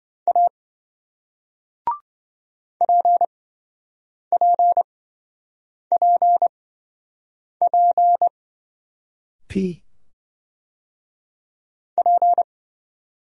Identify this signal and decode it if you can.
Morse code, heard as a keyed tone, about 27 wpm, spaced as for 7 wpm, 700 Hz. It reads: A PPPP P